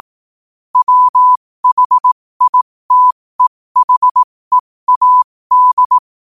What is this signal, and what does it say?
Morse code, heard as a keyed tone, about 18 wpm, spaced as for 15 wpm, 1000 Hz. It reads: WHITEHEAD
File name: qotdfiles / 15wpm